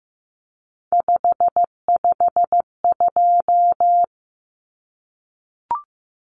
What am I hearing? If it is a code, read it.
Morse code, heard as a keyed tone, about 15 wpm, 700 Hz. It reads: 552
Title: Numbers - 3-Digit 15wpm